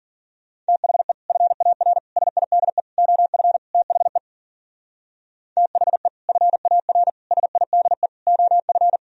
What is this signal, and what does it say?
Morse code, heard as a keyed tone, about 52 wpm, 700 Hz. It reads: THE FAR SIDE OF THE THE FAR SIDE OF